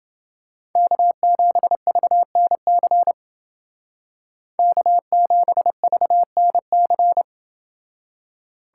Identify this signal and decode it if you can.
Morse code, heard as a keyed tone, about 28 wpm, 700 Hz. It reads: K7VNC K7VNC